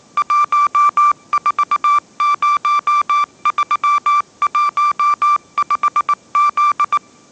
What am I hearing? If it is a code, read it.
Morse code, heard as a keyed tone, about 19 wpm, 1170 Hz. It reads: 140315Z